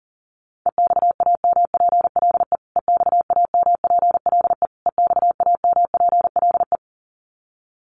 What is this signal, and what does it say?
Morse code, heard as a keyed tone, about 40 wpm, 700 Hz. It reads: EXAMPLE EXAMPLE EXAMPLE
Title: Top 500 Words Repeated 40wpm